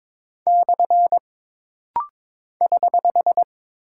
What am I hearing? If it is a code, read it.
Morse code, heard as a keyed tone, about 22 wpm, 700 Hz. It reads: / <HH>
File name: practice / 22wpm / Single Punctuation-Prosign - Rapid-Fire - 22wpm